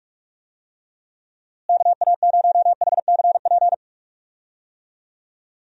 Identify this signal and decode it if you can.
Morse code, heard as a keyed tone, about 42 wpm, 700 Hz. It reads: KA0HCP
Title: USA Call Signs - 6 Characters - Repeated 3x - 45wpm